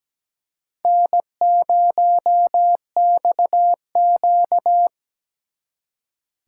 Morse code, heard as a keyed tone, about 20 wpm, 700 Hz. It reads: N0XQ
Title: Speed Racing - USA Call Signs 17wpm